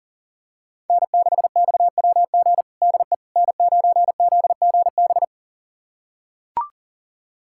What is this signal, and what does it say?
Morse code, heard as a keyed tone, about 40 wpm, 700 Hz. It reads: N6XWG DE N9ZGB
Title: USA Callsign DE Callsign - 5 Characters - 40wpm